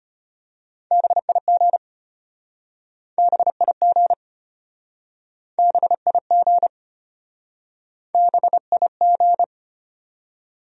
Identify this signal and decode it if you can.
Morse code, heard as a keyed tone, about 35 wpm, 700 Hz. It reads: BIG BIG BIG BIG